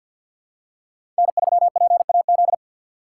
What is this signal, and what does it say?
Morse code, heard as a keyed tone, about 53 wpm, 700 Hz. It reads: N3PAZ
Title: USA Call Signs - 5 Characters - 4x Character Spacing - 50wpm